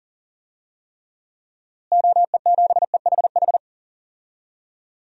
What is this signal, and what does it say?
Morse code, heard as a keyed tone, about 40 wpm, 700 Hz. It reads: OE7EHH
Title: DXCC-100 Call Signs - 6 Characters - Repeated 3x - 40wpm